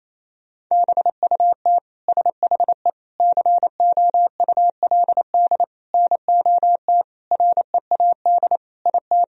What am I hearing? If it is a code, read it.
Morse code, heard as a keyed tone, about 28 wpm, 700 Hz. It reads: BUT SHE COULD NOT READ IT
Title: Sentences from Top 1000 Words - Review Part 4 - 28wpm